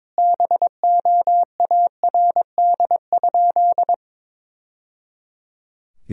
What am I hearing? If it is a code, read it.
Morse code, heard as a keyed tone, about 22 wpm, 700 Hz. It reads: BOARD?
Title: Sentences from Top 1000 Words - Review Part 4 - 22wpm